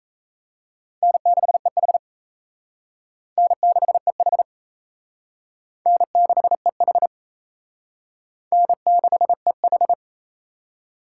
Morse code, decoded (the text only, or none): N6EH N6EH N6EH N6EH